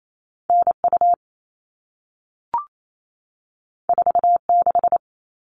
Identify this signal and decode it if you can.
Morse code, heard as a keyed tone, about 28 wpm, 700 Hz. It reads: NU 46
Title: All Cut Numbers - 2-Digit 28wpm